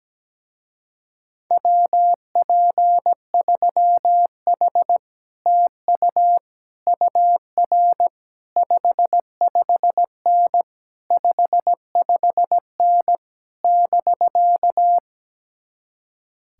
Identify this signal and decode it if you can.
Morse code, heard as a keyed tone, about 17 wpm, 700 Hz. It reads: WP3H TU UR 55N 55N <BK>
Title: Speed-Racing Common QSO Phrases - 17wpm